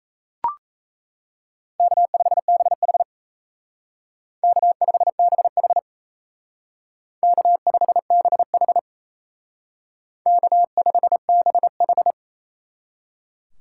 Morse code, heard as a keyed tone, about 33 wpm, 700 Hz. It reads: K5BH K5BH K5BH K5BH
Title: Speed Racing - USA Call Signs 28wpm